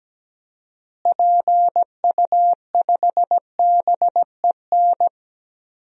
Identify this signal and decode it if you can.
Morse code, heard as a keyed tone, about 17 wpm, 700 Hz. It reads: PU5BEN
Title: DXCC-100 Call Signs - 6 Characters - Repeated 3x - 17wpm